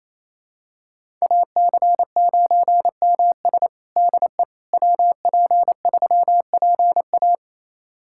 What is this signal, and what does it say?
Morse code, heard as a keyed tone, about 27 wpm, 700 Hz. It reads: AC9MS DE WP3PA
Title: USA Callsign DE Callsign - 5 Characters - 28wpm